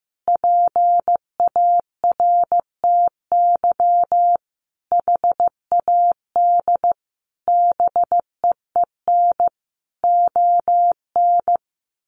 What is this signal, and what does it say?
Morse code, heard as a keyed tone, about 15 wpm, 700 Hz. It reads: PARTY HAD BEEN ON
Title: Sentences from Top 1400 Words 15wpm